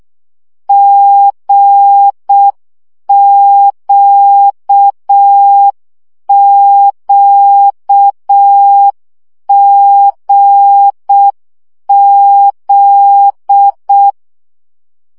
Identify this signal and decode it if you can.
Morse code, heard as a keyed tone, about 6 wpm, 800 Hz. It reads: GQQGZ
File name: CW gyakorlas g z q 5 baud